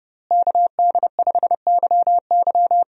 Morse code, heard as a keyed tone, about 30 wpm, 700 Hz. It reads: KD5YY
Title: USA Call Signs - 5 Characters - Repeated 2x - 30wpm